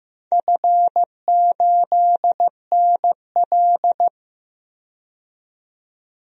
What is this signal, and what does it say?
Morse code, heard as a keyed tone, about 15 wpm, 700 Hz. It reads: F8NL
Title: DXCC-100 Call Signs - 4 Characters - 15wpm